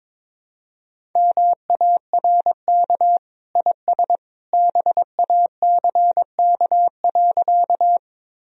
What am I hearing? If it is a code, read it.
Morse code, heard as a keyed tone, about 22 wpm, 700 Hz. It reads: MARK IS BACK.
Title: Sentences from Top 1000 Words - Review Part 1 - 22wpm